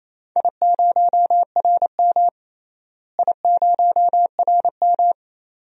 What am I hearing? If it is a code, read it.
Morse code, heard as a keyed tone, about 28 wpm, 700 Hz. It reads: I0RM I0RM
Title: DXCC-100 Call Signs - 4 Characters - Repeated 2x - 28wpm